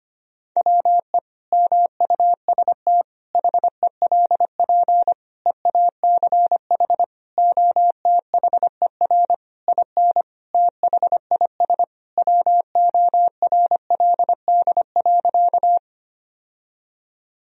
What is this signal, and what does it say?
Morse code, heard as a keyed tone, about 25 wpm, 700 Hz. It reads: WE MUST HELP EACH OTHER IN THIS WORLD.
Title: Sentences from Top 1000 Words - Review Part 1 - 25wpm